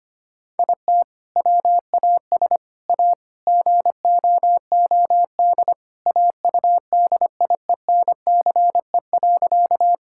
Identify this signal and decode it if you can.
Morse code, heard as a keyed tone, about 25 wpm, 700 Hz. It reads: IT WAS A GOOD AUDIENCE.
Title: Sentences from Top 1000 Words - Review Part 5 - 25wpm